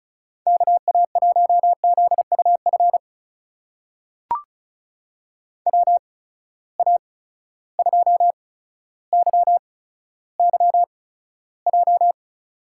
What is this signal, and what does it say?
Morse code, heard as a keyed tone, about 36 wpm, spaced as for 15 wpm, 700 Hz. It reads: KA1ZUF WA2YYJ
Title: USA Call Signs - 6 Characters - 8x Character Spacing - 35wpm